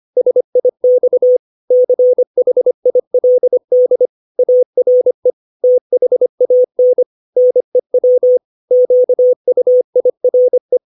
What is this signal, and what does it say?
Morse code, heard as a keyed tone, about 25 wpm, 500 Hz. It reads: SIX CHILD ARE THAN NEW QUIRE
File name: 632 words / 632 Words 25WPM